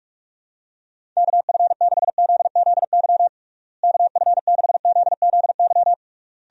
Morse code, heard as a keyed tone, about 46 wpm, 700 Hz. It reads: KF6ZZY KF6ZZY
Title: USA Call Signs - 6 Characters - Repeated 3x - 45wpm